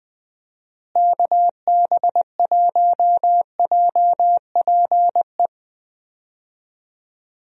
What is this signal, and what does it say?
Morse code, heard as a keyed tone, about 20 wpm, 700 Hz. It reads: KB1JPE
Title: USA Call Signs - 6 Characters - Repeated 3x - 20wpm